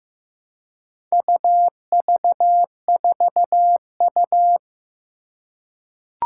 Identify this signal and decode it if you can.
Morse code, heard as a keyed tone, about 15 wpm, 700 Hz. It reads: UV4U
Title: Four Challenging Letter-Number Combinations 15wpm